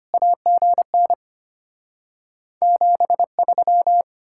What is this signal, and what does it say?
Morse code, heard as a keyed tone, about 33 wpm, 700 Hz. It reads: AGN 73